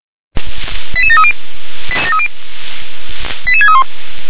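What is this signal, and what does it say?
Five-tone selective call (ZVEI), heard as quick runs of five tones, tones of about 70 ms. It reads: 88730 88631
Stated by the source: ZVEI five-tone selective call received on an SDR.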